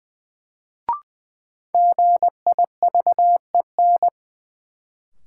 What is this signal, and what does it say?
Morse code, heard as a keyed tone, about 21 wpm, 700 Hz. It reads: GIVEN
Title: Sentences from Top 1400 Words - New Word Drill Alt - 20wpm